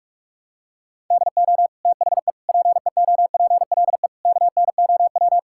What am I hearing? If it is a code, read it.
Morse code, heard as a keyed tone, about 46 wpm, 700 Hz. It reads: DO THE PEOPLE KNOW
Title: Sentences from Top 1000 Words - Review Part 5 - 45wpm